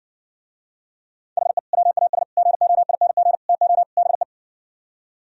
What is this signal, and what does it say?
Morse code, heard as a keyed tone, about 60 wpm, 700 Hz. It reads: HE WAS GOING TO BE